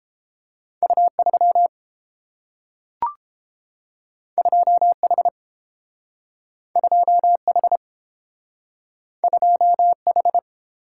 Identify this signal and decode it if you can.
Morse code, heard as a keyed tone, about 30 wpm, 700 Hz. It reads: U3 2H 2H 2H